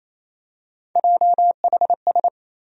Morse code, heard as a keyed tone, about 27 wpm, 700 Hz. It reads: JHS
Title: USA Call Sign Postfixes - 3 Letters 28wpm